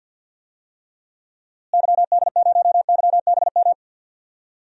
Morse code, heard as a keyed tone, about 52 wpm, 700 Hz. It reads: YD0YBM
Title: DXCC-100 Call Signs - 6 Characters - Repeated 3x - 50wpm